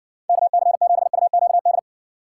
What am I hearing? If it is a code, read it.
Morse code, heard as a keyed tone, about 64 wpm, 700 Hz. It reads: KC7UQD